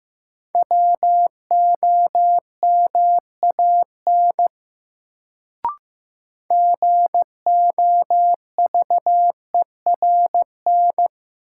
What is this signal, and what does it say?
Morse code, heard as a keyed tone, about 15 wpm, 700 Hz. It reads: WOMAN GOVERN
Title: Sentences from Top 1000 Words - Review Part 2 - New Word Drill - 15wpm